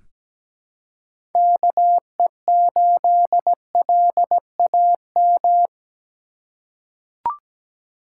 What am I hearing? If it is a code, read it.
Morse code, heard as a keyed tone, about 17 wpm, 700 Hz. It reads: KE8LAM
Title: USA Call Signs - 6 Characters - 4x Character Spacing - 17wpm